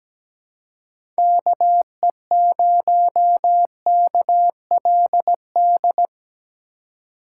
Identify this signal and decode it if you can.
Morse code, heard as a keyed tone, about 17 wpm, 700 Hz. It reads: KE0KLD
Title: USA Call Signs - 6 Characters - 4x Character Spacing - 17wpm